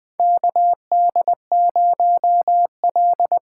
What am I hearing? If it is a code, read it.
Morse code, heard as a keyed tone, about 20 wpm, 700 Hz. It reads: KD0L
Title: USA Call Signs - 6 Characters - 8x Character Spacing - 20wpm